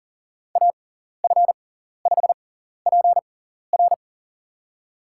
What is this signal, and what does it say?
Morse code, heard as a keyed tone, about 40 wpm, spaced as for 14 wpm, 700 Hz. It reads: AF5PR
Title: USA Call Signs - 5 Characters - 6x Character Spacing - 40wpm